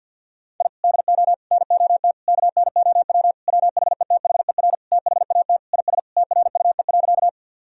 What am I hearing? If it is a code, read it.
Morse code, heard as a keyed tone, about 50 wpm, 700 Hz. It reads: I DO NOT KNOW WHETHER THAT IS TRUE.
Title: Sentences from Top 1000 Words - Review Part 3 - 50wpm